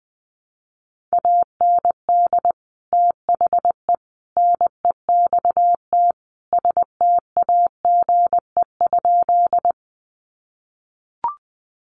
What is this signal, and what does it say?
Morse code, heard as a keyed tone, about 20 wpm, 700 Hz. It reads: AND THE NEXT STAGE?
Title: Sentences from Top 1200 Words 20wpm